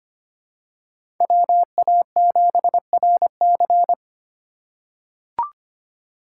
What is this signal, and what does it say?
Morse code, heard as a keyed tone, about 25 wpm, 700 Hz. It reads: WA7RC
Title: USA Call Signs - 5 Characters - 4x Character Spacing - 25wpm